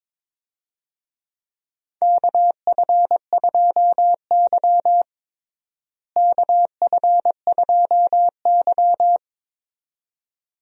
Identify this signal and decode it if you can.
Morse code, heard as a keyed tone, about 22 wpm, 700 Hz. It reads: KF2Y KF2Y
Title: USA Call Signs - 4 Characters - Repeated 2x - 22wpm